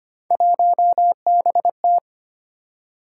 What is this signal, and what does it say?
Morse code, heard as a keyed tone, about 25 wpm, 700 Hz. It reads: 1BT
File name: Three Letter-Number Combinations - 25wpm